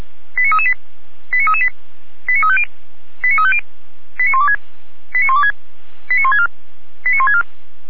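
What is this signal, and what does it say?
Five-tone selective call (ZVEI), heard as quick runs of five tones, tones of about 70 ms. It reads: 89338 89338 89350 89350 89126 89126 89164 89164
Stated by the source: ZVEI five-tone call 89401, received by SDR.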